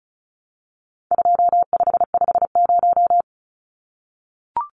Morse code, heard as a keyed tone, about 34 wpm, 700 Hz. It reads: 2550
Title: Numbers - 4-Digit 35wpm